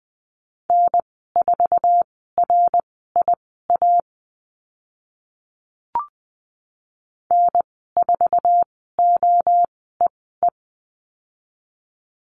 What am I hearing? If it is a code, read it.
Morse code, heard as a keyed tone, about 20 wpm, spaced as for 14 wpm, 700 Hz. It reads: N4RIA N4OEE
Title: 076 Lesson 13g - Letter L - Call Signs 20wpm